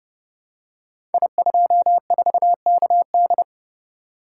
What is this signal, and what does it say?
Morse code, heard as a keyed tone, about 28 wpm, 700 Hz. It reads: I24KD
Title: Code Groups - 5 Letter:Numbers - 30wpm